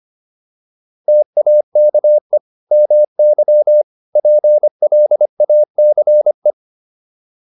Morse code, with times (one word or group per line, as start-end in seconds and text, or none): 1.1-2.4 TAKE
2.7-3.8 MY
4.2-6.5 PLACE